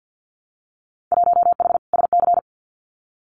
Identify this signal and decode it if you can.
Morse code, heard as a keyed tone, about 50 wpm, 700 Hz. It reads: 1H SC
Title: Winter Field Day Exchanges 50wpm